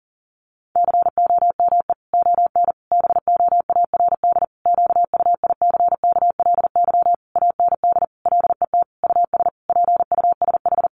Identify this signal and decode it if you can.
Morse code, heard as a keyed tone, about 40 wpm, 700 Hz. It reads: COME ON BOARD QUICKLY AND LET US PUSH
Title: Sentences from Top 1300 Words 40wpm